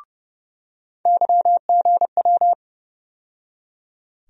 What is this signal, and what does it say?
Morse code, heard as a keyed tone, about 31 wpm, 700 Hz. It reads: YGW
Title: USA Call Sign Postfixes - 3 Letters 30wpm